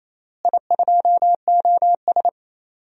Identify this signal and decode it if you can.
Morse code, heard as a keyed tone, about 28 wpm, 700 Hz. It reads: I2OS
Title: DXCC-100 Call Signs - 4 Characters - Repeated 3x - 28wpm